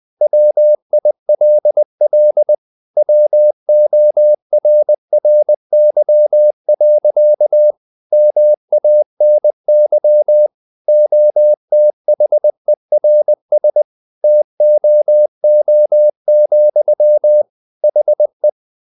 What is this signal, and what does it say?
Morse code, heard as a keyed tone, about 20 wpm, 600 Hz. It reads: WILL WORRY. MANY OTHERS TOO, HE
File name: the old man and the sea part 2 20wpm